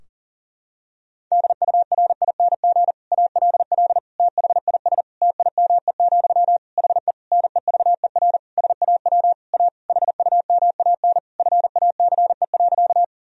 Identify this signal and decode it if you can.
Morse code, heard as a keyed tone, about 37 wpm, 700 Hz. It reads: DURING ALL THIS TIME, HE NEVER SAW A HUMAN FACE.
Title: Sentences from Top 1000 Words - Review Part 3 - 40wpm